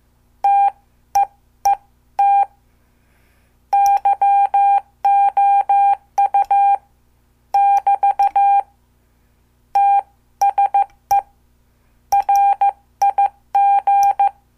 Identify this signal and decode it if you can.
Morse code, heard as a keyed tone, about 14 wpm, spaced as for 13 wpm, 800 Hz. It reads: TEET YOU <BT> TSE RIG
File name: morse / qso